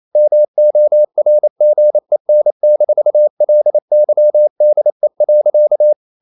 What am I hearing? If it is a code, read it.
Morse code, heard as a keyed tone, about 28 wpm, 600 Hz. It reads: MORGEN-LYDE.
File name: Ferie QRQ 28wpm